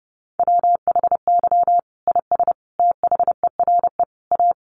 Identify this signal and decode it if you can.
Morse code, heard as a keyed tone, about 30 wpm, 700 Hz. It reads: WHY IS THERE A
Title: Sentences from Top 1200 Words 30wpm